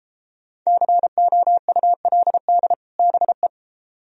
Morse code, heard as a keyed tone, about 33 wpm, 700 Hz. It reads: COULD BE